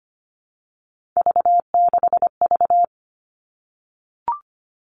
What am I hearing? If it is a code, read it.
Morse code, heard as a keyed tone, about 25 wpm, 700 Hz. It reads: V6V
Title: All Cut Numbers - 3-Digit 25wpm